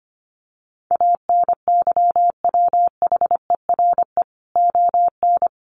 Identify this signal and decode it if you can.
Morse code, heard as a keyed tone, about 25 wpm, 700 Hz. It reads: ANYWHERE ON
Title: Sentences from Top 1400 Words 25wpm